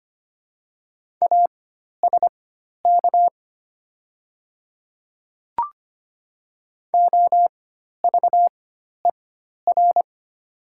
Morse code, heard as a keyed tone, about 25 wpm, spaced as for 12 wpm, 700 Hz. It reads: ASK OVER
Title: 4x Character Spacing - Top 300 Words - 25wpm